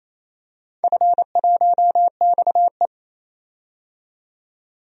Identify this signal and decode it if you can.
Morse code, heard as a keyed tone, about 27 wpm, 700 Hz. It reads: F1XE
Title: DXCC-100 Call Signs - 4 Characters - 28wpm